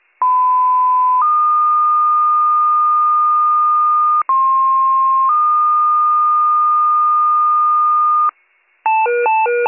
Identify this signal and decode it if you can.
Two-tone page, heard as a one-second tone, then a three-second tone, about 1010 then 1220 Hz, about 1060 then 1220 Hz. It is heard from 0.2 to 8.3 s.